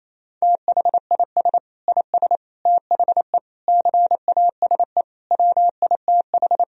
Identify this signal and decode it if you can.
Morse code, heard as a keyed tone, about 28 wpm, 700 Hz. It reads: THIS IS THE CASE WITH
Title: Sentences from Top 1000 Words - Review Part 3 - 28wpm